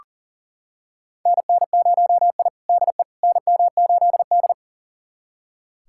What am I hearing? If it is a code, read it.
Morse code, heard as a keyed tone, about 37 wpm, 700 Hz. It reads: NN0I DE NM8D
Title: USA Callsign DE Callsign - 4 Characters - 40wpm